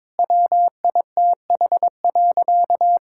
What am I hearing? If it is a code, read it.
Morse code, heard as a keyed tone, about 22 wpm, 700 Hz. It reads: WITH.
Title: Sentences from Top 1000 Words - Review Part 5 - 22wpm